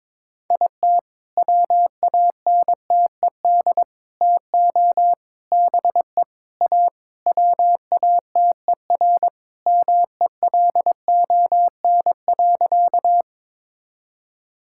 Morse code, heard as a keyed tone, about 22 wpm, 700 Hz. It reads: IT WANTED TO BE A WATER MELON.
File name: All-Age Appropriate Jokes - 22wpm